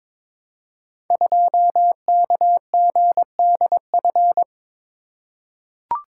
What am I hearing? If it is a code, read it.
Morse code, heard as a keyed tone, about 23 wpm, 700 Hz. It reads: 2KGDF
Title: Code Groups - 5 Letter:Numbers - 22wpm